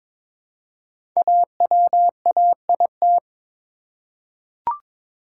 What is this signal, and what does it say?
Morse code, heard as a keyed tone, about 22 wpm, 700 Hz. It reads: AWAIT
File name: 5 Letter Words - First 3 Letters Given - 22wpm